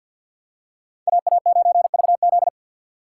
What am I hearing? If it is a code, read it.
Morse code, heard as a keyed tone, about 52 wpm, 700 Hz. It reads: AA9VZ
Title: USA Call Signs - 2x Character Spacing - 50wpm